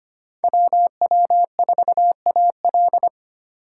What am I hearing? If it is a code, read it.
Morse code, heard as a keyed tone, about 31 wpm, 700 Hz. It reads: WW4AL
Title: Speed Racing - USA 2x2 Call Signs 25wpm